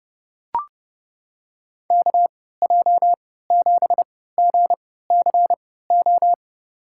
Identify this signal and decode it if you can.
Morse code, heard as a keyed tone, about 30 wpm, spaced as for 17 wpm, 700 Hz. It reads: KJ7GCO
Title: USA Call Signs - 6 Characters - 3x Character Spacing - 30wpm